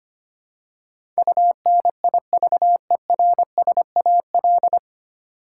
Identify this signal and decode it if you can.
Morse code, heard as a keyed tone, about 25 wpm, 700 Hz. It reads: UNIVERSAL
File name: practice / 25wpm / 9 Letter Words - First 3 Letters Given - 25wpm